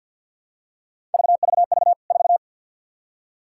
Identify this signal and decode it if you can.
Morse code, heard as a keyed tone, about 50 wpm, 700 Hz. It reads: VVV 4
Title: K1USN SST QSO Practice 50wpm